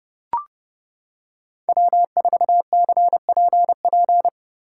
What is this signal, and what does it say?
Morse code, heard as a keyed tone, about 31 wpm, 700 Hz. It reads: W4CPP